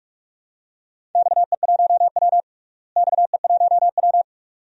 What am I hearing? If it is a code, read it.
Morse code, heard as a keyed tone, about 43 wpm, 700 Hz. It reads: XE1W XE1W
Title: DXCC-100 Call Signs - 4 Characters - Repeated 2x - 45wpm